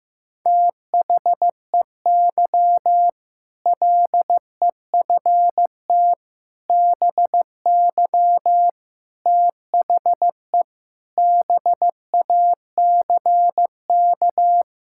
Morse code, heard as a keyed tone, about 15 wpm, 700 Hz. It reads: THEY LEFT BY THE BACK
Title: Sentences from Top 1600 Words 15wpm